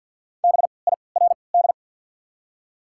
Morse code, heard as a keyed tone, about 52 wpm, spaced as for 29 wpm, 690 Hz. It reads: BIRD
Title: 3x Character Spacing - Top 500 Words - 50wpm